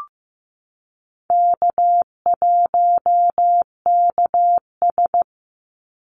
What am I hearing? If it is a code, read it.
Morse code, heard as a keyed tone, about 15 wpm, 700 Hz. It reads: K1KS
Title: Easy Call Signs Repeated 2x - 15wpm